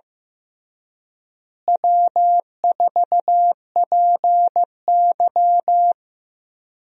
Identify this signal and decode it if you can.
Morse code, heard as a keyed tone, about 15 wpm, 700 Hz. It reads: W4PY
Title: USA Call Signs - 4 Characters - Repeated 3x - 15wpm